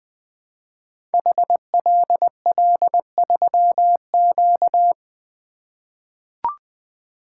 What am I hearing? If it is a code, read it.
Morse code, heard as a keyed tone, about 20 wpm, 700 Hz. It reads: HLL3Q
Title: Code Groups - 5 Letter:Numbers - 20wpm